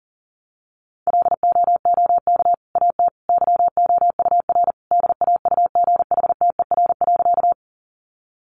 Morse code, heard as a keyed tone, about 40 wpm, 700 Hz. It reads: LOOK AT YOUR DAUGHTER.
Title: Sentences from Top 1500 Words 40wpm